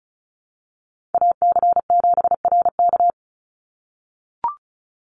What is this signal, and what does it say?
Morse code, heard as a keyed tone, about 34 wpm, 700 Hz. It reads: AC7RK
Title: USA Call Signs 35wpm